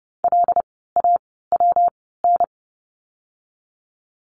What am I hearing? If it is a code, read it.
Morse code, heard as a keyed tone, about 30 wpm, 700 Hz. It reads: L A W N
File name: course / 30:10wpm / 115 Lesson 19f - Letter w - Words 30wpm